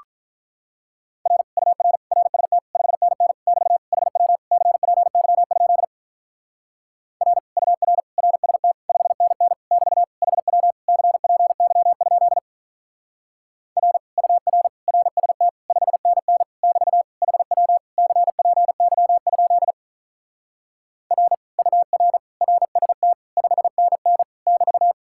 Morse code, heard as a keyed tone, about 44 wpm, 700 Hz. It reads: R UR RST 5NN <BT> HW CPY? R UR RST 5NN <BT> HW CPY? R UR RST 5NN <BT> HW CPY? R UR RST 5NN <BT>